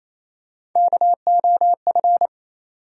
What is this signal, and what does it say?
Morse code, heard as a keyed tone, about 27 wpm, 700 Hz. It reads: KOF
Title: USA Call Sign Postfixes - 3 Letters 28wpm